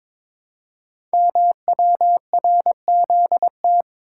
Morse code, heard as a keyed tone, about 22 wpm, 700 Hz. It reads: MWRZT
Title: Code Groups - 5 Letters - 22wpm